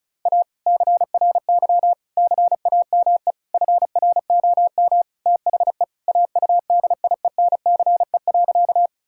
Morse code, heard as a keyed tone, about 36 wpm, 700 Hz. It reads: A CRY CAME FROM THE AUDIENCE.
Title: Sentences from Top 1000 Words - Review Part 5 - 35wpm